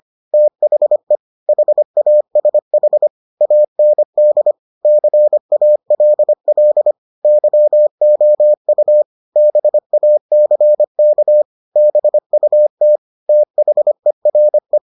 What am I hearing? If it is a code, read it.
Morse code, heard as a keyed tone, about 25 wpm, 600 Hz. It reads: THE HASH AND CALL YOU BACK BUT THERE